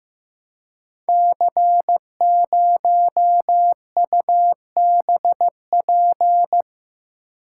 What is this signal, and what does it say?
Morse code, heard as a keyed tone, about 15 wpm, 700 Hz. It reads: C0UBP
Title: Code Groups - 5 Letter:Numbers - 15wpm